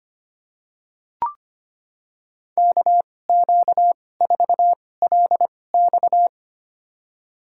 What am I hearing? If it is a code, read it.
Morse code, heard as a keyed tone, about 25 wpm, spaced as for 18 wpm, 700 Hz. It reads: KQ4LX
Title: USA Call Signs - 5 Characters - 2x Character Spacing - 25wpm